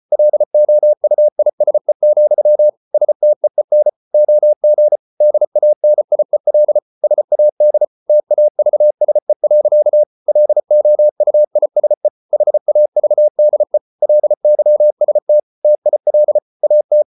Morse code, heard as a keyed tone, about 34 wpm, 600 Hz. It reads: LOUISE, STEEN OG DANIEL SAD TAVSE. LOUISE HAVDE LYST TIL AT
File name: Ferie QRQ 34wpm